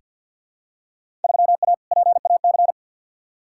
Morse code, heard as a keyed tone, about 47 wpm, 700 Hz. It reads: 3A PAC
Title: Field Day Exchanges - 50wpm